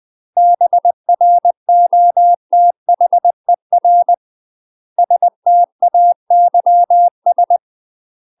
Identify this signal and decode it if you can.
Morse code, heard as a keyed tone, about 20 wpm, 700 Hz. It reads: BROTHER STAYS
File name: UK MASTER QSOs TRAINING/20WPM